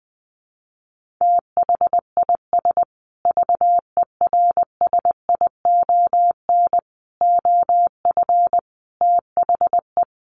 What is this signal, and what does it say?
Morse code, heard as a keyed tone, about 20 wpm, 700 Hz. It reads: THIS VERSION OF THE
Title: Sentences from Top 1400 Words 20wpm